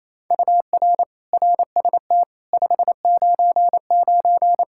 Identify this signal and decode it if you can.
Morse code, heard as a keyed tone, about 28 wpm, 700 Hz. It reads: UR RST 599
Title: Practice QSO - Exchange 1 - Beginning - 28wpm